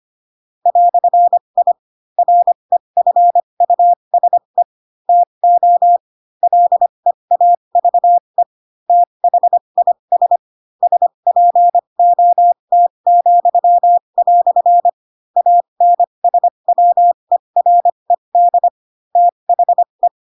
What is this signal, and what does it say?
Morse code, heard as a keyed tone, about 25 wpm, spaced as for 22 wpm, 700 Hz. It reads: "I REFUSE TO LEAVE THIS SPOT," ANSWERED THE
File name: Pinocchio / Pinocchio ch4 18 25 ik1hge